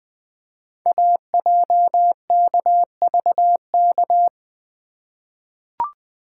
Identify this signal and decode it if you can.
Morse code, heard as a keyed tone, about 20 wpm, 700 Hz. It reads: AJKVK
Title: Code Groups - 5 Letters - 20wpm